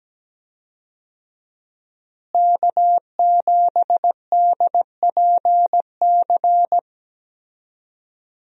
Morse code, heard as a keyed tone, about 17 wpm, 700 Hz. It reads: K7DPC
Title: USA Call Signs - 5 Characters - Repeated 2x - 17wpm